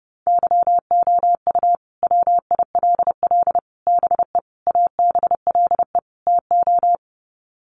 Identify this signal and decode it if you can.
Morse code, heard as a keyed tone, about 30 wpm, 700 Hz. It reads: YOU WILL BE ABLE TO
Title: Sets of 5 Words Encore2 30wpm